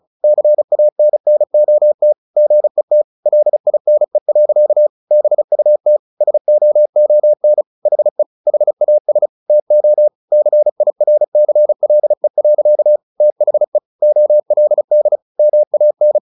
Morse code, heard as a keyed tone, about 34 wpm, 600 Hz. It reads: CANNOT GET LINE. BUT SOON HE HAS TO CIRCLE. THE OLD MAN
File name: the old man and the sea 2 35wpm